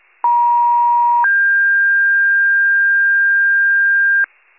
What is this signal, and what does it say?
Two-tone page, heard as a one-second tone, then a three-second tone, about 950 then 1600 Hz, with noise. It is heard from 0.2 to 4.3 s.